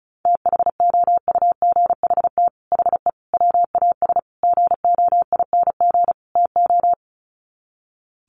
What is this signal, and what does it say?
Morse code, heard as a keyed tone, about 34 wpm, 700 Hz. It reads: THOUGHT HE WAS GOING TO
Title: Sets of 5 Words Encore2 35wpm